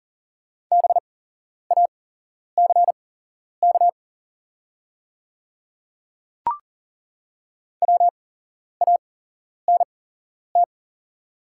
Morse code, heard as a keyed tone, about 40 wpm, spaced as for 11 wpm, 700 Hz. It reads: BACK WANT
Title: 8x Character Spacing - Top 200 Words - 40wpm